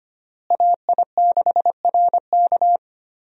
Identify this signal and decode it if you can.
Morse code, heard as a keyed tone, about 25 wpm, 700 Hz. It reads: AI6RK
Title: USA Call Signs - 5 Characters - Repeated 3x - 25wpm